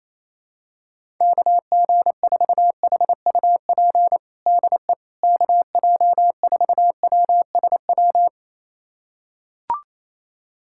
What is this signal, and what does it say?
Morse code, heard as a keyed tone, about 28 wpm, 700 Hz. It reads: KG4HUP DE KJ4WSW
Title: USA Callsign DE Callsign - 6 Characters - 28wpm